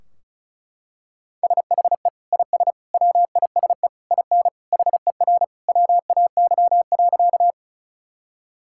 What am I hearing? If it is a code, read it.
Morse code, heard as a keyed tone, about 34 wpm, 700 Hz. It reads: SHE IS WISE IN HER WAY.
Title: Sentences from Top 1600 Words 35wpm